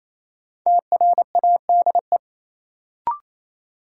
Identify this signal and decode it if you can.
Morse code, heard as a keyed tone, about 28 wpm, 700 Hz. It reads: TRADE
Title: Sets of 5 Words Encore - New Word Drill - 28wpm